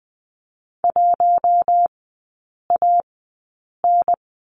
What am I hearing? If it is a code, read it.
Morse code, heard as a keyed tone, about 20 wpm, spaced as for 8 wpm, 700 Hz. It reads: 1AN
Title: 041 Lesson 8e - Number 1 - Three Characters 20wpm